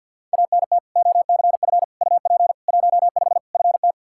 Morse code, heard as a keyed tone, about 52 wpm, 700 Hz. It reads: ANT OCF UP 15 FT
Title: Practice QSO - Exchange 2 - Info - 50wpm